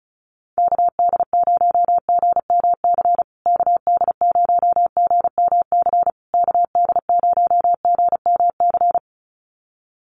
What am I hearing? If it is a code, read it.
Morse code, heard as a keyed tone, about 34 wpm, 700 Hz. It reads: KD0GMC KD0GMC KD0GMC
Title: USA Call Signs Repeated 3x 35wpm